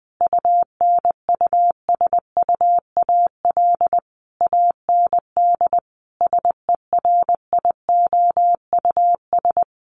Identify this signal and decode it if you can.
Morse code, heard as a keyed tone, about 20 wpm, 700 Hz. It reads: UNUSUAL AND SERIOUS
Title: Sentences from Top 1400 Words 20wpm